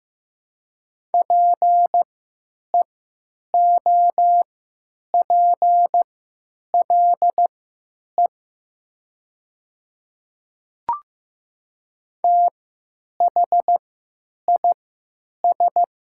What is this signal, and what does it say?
Morse code, heard as a keyed tone, about 15 wpm, spaced as for 9 wpm, 690 Hz. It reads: PEOPLE THIS
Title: 3x Character Spacing - Top 100 Words - 15wpm